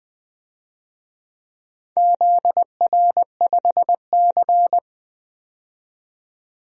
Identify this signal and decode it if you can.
Morse code, heard as a keyed tone, about 21 wpm, 700 Hz. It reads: ZR5C
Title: DXCC-100 Call Signs - 4 Characters - 20wpm